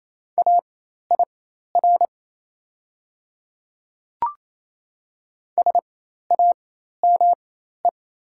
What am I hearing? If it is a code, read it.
Morse code, heard as a keyed tone, about 28 wpm, spaced as for 13 wpm, 700 Hz. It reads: AIR SAME
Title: 4x Character Spacing - Top 200 Words - 28wpm